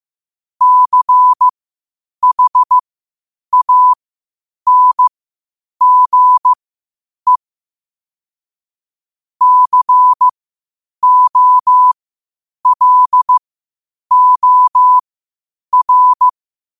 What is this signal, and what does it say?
Morse code, heard as a keyed tone, about 15 wpm, spaced as for 8 wpm, 1000 Hz. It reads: CHANGE COLOR
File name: qotdfiles / 8wpm